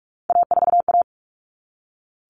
Morse code, heard as a keyed tone, about 44 wpm, 700 Hz. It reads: A4A
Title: All Cut Numbers - 3-Digit 45wpm